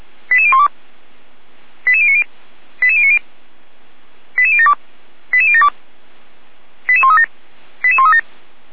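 Five-tone selective call (ZVEI), heard as quick runs of five tones, tones of about 70 ms. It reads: 88012 88090 88090 88072 88072 88137 88137
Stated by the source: ZVEI five-tone call received by SDR.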